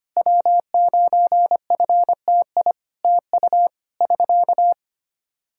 Morse code, heard as a keyed tone, about 25 wpm, 700 Hz. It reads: W9FTI TU <SK>
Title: Practice QSO - Exchange 4 - Ending - 25wpm